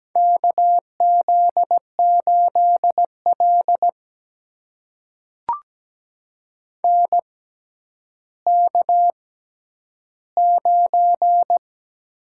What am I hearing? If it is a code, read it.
Morse code, heard as a keyed tone, about 17 wpm, spaced as for 10 wpm, 700 Hz. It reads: KZ8L NK9
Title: USA Call Signs - 4 Characters - 6x Character Spacing - 17wpm